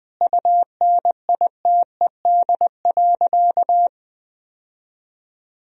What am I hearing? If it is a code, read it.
Morse code, heard as a keyed tone, about 21 wpm, 700 Hz. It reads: UNITED.
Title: Sentences from Top 1000 Words - Review Part 4 - 20wpm